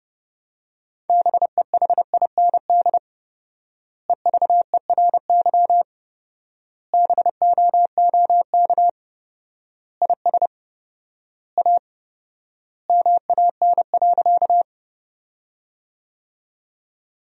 Morse code, heard as a keyed tone, about 30 wpm, 700 Hz. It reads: BEHIND EVERY BOOK IS A MAN.
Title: 4x Word-Spacing - Sentences from Top 500 Words - 30wpm